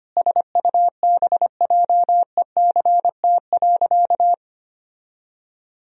Morse code, heard as a keyed tone, about 25 wpm, 700 Hz. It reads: SUBJECT.
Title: Sentences from Top 1000 Words - Review Part 4 - 25wpm